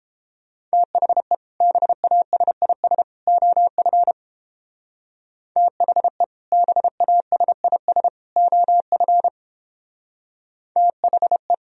Morse code, heard as a keyed tone, about 29 wpm, 700 Hz. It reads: THE BASIS OF THE BASIS OF THE